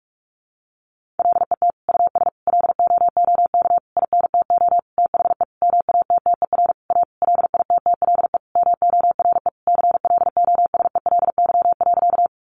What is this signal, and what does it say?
Morse code, heard as a keyed tone, about 44 wpm, 700 Hz. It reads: LET US LOOK INTO THE MATTER A LITTLE MORE CLOSELY.
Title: Sentences from Top 1400 Words 45wpm